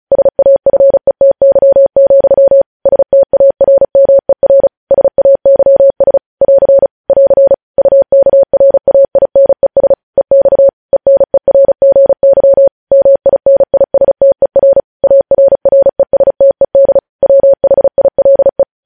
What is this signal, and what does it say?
Morse code, heard as a keyed tone, about 36 wpm, 550 Hz. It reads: SAFETY, STARMER SAYS <AR> <AR> UKRAINES EX ENERGY MINISTER ARRESTED WHILE